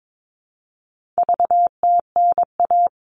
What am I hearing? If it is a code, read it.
Morse code, heard as a keyed tone, about 22 wpm, 700 Hz. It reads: VTNA
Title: All Cut Numbers - 4-Digit 22wpm